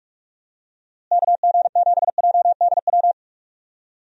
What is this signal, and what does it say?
Morse code, heard as a keyed tone, about 46 wpm, 700 Hz. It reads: KG7JDW
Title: USA Call Signs - 6 Characters - Repeated 2x - 45wpm